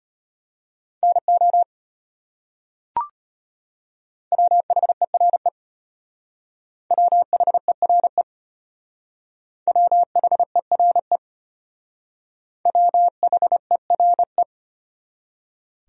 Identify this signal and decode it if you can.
Morse code, heard as a keyed tone, about 32 wpm, 700 Hz. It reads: NO WHERE WHERE WHERE WHERE